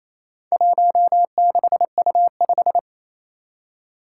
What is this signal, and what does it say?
Morse code, heard as a keyed tone, about 27 wpm, 700 Hz. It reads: 16U5
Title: Four Challenging Letter-Number Combinations 28wpm